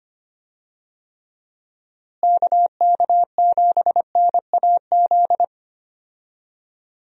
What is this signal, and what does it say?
Morse code, heard as a keyed tone, about 25 wpm, 700 Hz. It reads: KK7NAZ
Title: USA Call Signs - 6 Characters - Repeated 3x - 25wpm